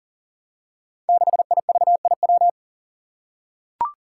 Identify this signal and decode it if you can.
Morse code, heard as a keyed tone, about 38 wpm, 700 Hz. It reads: 6IVIW
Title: Five Challenging Letter-Number Combinations 40wpm